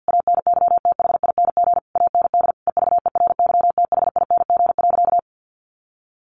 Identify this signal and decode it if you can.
Morse code, heard as a keyed tone, about 50 wpm, 700 Hz. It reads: ANYTHING AND EVERYTHING.
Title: Sentences from Top 1200 Words 50wpm